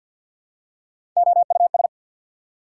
Morse code, heard as a keyed tone, about 51 wpm, 690 Hz. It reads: OUS
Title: Prefixes and Suffixes 50wpm